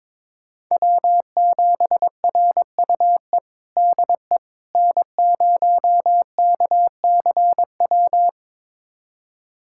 Morse code, heard as a keyed tone, about 22 wpm, 700 Hz. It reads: W7RUE DE N0KCW
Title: USA Callsign DE Callsign - 5 Characters - 22wpm